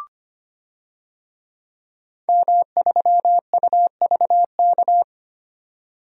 Morse code, heard as a keyed tone, about 25 wpm, 700 Hz. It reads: M3UVK
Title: DXCC-100 Call Signs - 5 Characters - 25wpm